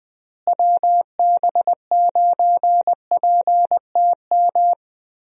Morse code, heard as a keyed tone, about 20 wpm, 700 Hz. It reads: WB9PTM
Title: USA Call Signs - 6 Characters - Repeated 3x - 20wpm